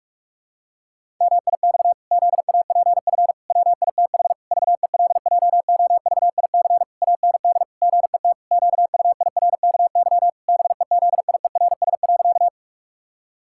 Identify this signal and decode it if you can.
Morse code, heard as a keyed tone, about 48 wpm, 700 Hz. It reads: MIX ZAPF WITH VELJOVIC AND GET QUIRKY BEZIERS.
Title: Pangram Challenge v2 - 45wpm